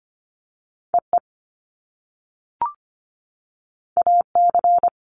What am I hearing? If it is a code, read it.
Morse code, heard as a keyed tone, about 25 wpm, 700 Hz. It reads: EE AC
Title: Top Two Letter Combinations in Words 25wpm